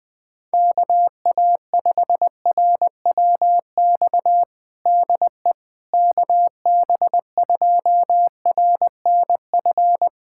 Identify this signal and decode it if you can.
Morse code, heard as a keyed tone, about 20 wpm, 700 Hz. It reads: KA5RWX DE KB2RNF
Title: USA Callsign DE Callsign - 6 Characters - 20wpm